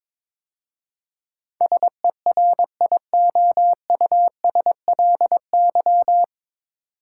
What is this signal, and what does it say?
Morse code, heard as a keyed tone, about 22 wpm, 690 Hz. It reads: SERIOUSLY
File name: 9 Letter Words - First 3 Letters Given - 22wpm